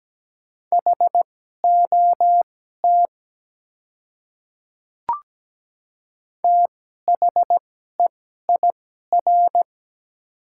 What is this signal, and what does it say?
Morse code, heard as a keyed tone, about 17 wpm, spaced as for 12 wpm, 700 Hz. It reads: HOT THEIR
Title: 2x Character Spacing - Top 300 Words - 17wpm